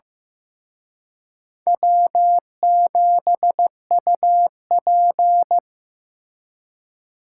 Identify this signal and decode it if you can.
Morse code, heard as a keyed tone, about 18 wpm, 700 Hz. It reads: W7UP
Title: Speed Racing - USA 1x2 Call Signs 15wpm